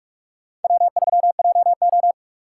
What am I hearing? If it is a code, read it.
Morse code, heard as a keyed tone, about 46 wpm, 700 Hz. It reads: W3JO
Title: USA Call Signs - 3x Character Spacing - 45wpm